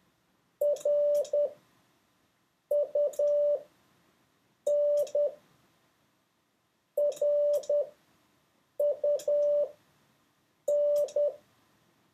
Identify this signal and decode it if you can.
Morse code, heard as a keyed tone, about 10 wpm, spaced as for 6 wpm, 580 Hz. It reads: RUN RUN
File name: 4of6-words-1